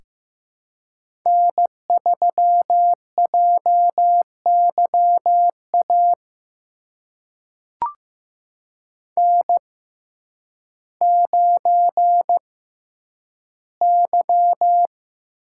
Morse code, heard as a keyed tone, about 15 wpm, spaced as for 9 wpm, 700 Hz. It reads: N3JYA N9Y
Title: USA Call Signs - 5 Characters - 6x Character Spacing - 15wpm